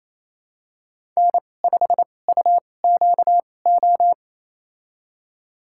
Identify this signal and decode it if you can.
Morse code, heard as a keyed tone, about 28 wpm, spaced as for 20 wpm, 700 Hz. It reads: N5UQO
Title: USA Call Signs - 5 Characters - 2x Character Spacing - 28wpm